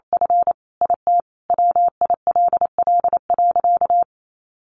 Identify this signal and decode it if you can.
Morse code, heard as a keyed tone, about 28 wpm, 700 Hz. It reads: F IT WILL.
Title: Sentences from Top 1500 Words 28wpm